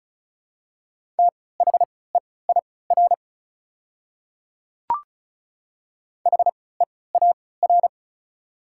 Morse code, heard as a keyed tone, about 36 wpm, spaced as for 20 wpm, 690 Hz. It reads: THEIR HEAR
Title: 3x Character Spacing - Top 300 Words - 35wpm